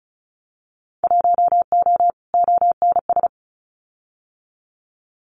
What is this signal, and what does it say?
Morse code, heard as a keyed tone, about 34 wpm, 700 Hz. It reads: 1O ONS
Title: Winter Field Day Exchanges Encore 35wpm